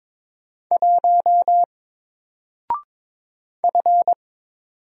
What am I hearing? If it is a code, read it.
Morse code, heard as a keyed tone, about 22 wpm, 700 Hz. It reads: 1 F
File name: Single-Letter-Number-22wpm